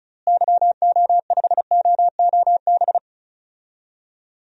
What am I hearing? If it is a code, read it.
Morse code, heard as a keyed tone, about 36 wpm, 700 Hz. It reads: YO5OOB
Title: DXCC-100 Call Signs - 6 Characters - Repeated 3x - 35wpm